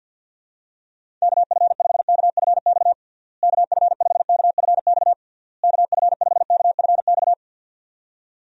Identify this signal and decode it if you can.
Morse code, heard as a keyed tone, about 50 wpm, 700 Hz. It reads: KF5KFX KF5KFX KF5KFX
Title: USA Call Signs - 6 Characters - Repeated 3x - 50wpm